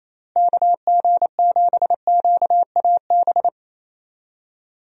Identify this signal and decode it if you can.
Morse code, heard as a keyed tone, about 28 wpm, 700 Hz. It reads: KG7QAB
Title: USA Call Signs - 6 Characters - Repeated 3x - 28wpm